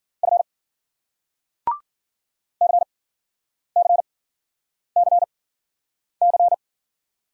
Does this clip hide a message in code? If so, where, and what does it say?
Yes: Morse code from 0.2 to 6.5 s, heard as a keyed tone, about 53 wpm, spaced as for 9 wpm, 700 Hz. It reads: F CCCC